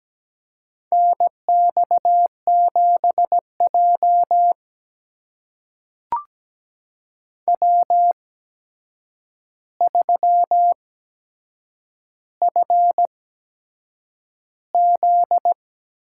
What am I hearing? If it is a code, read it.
Morse code, heard as a keyed tone, about 17 wpm, spaced as for 7 wpm, 700 Hz. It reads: NX7J W3FZ
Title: USA Call Signs - 4 Characters - 8x Character Spacing - 17wpm